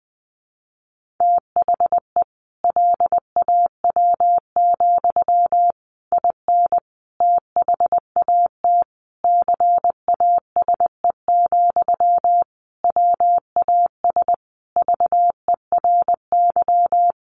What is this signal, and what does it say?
Morse code, heard as a keyed tone, about 20 wpm, 700 Hz. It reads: THE LAW, IN THAT CASE, WAS VERY
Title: Sentences from Top 1400 Words 20wpm